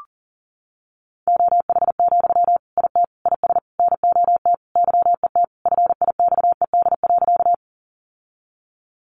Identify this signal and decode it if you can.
Morse code, heard as a keyed tone, about 40 wpm, 700 Hz. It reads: OH, IT IS NOT YET FIXED.
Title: Sentences from Top 1100 Words 40wpm